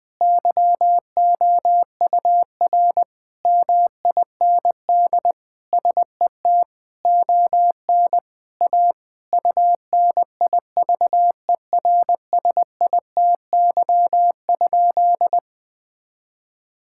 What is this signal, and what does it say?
Morse code, heard as a keyed tone, about 20 wpm, 700 Hz. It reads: YOUR MIND SET ON A UNIVERSITY?
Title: Sentences from Top 1000 Words - Review Part 5 - 20wpm